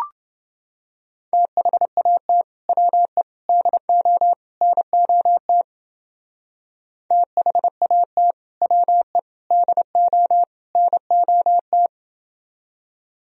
Morse code, heard as a keyed tone, about 29 wpm, 700 Hz. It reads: THAT WE DO NOT THAT WE DO NOT